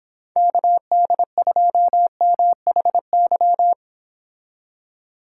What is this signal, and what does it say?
Morse code, heard as a keyed tone, about 29 wpm, 700 Hz. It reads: KD2MHY